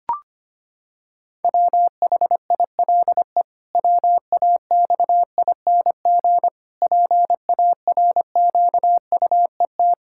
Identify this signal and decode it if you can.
Morse code, heard as a keyed tone, about 25 wpm, 700 Hz. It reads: WHILE WAXING PARQUET
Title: Pangram Challenge v2 - 25wpm